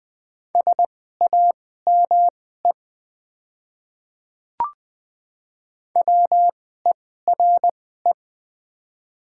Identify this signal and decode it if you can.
Morse code, heard as a keyed tone, about 20 wpm, spaced as for 15 wpm, 700 Hz. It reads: SAME WERE
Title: 2x Character Spacing - Top 500 Words - 20wpm